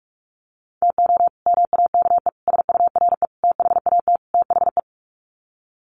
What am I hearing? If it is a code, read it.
Morse code, heard as a keyed tone, about 44 wpm, 700 Hz. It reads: TO MAKE SURE THAT THE
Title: Sets of 5 Words 45wpm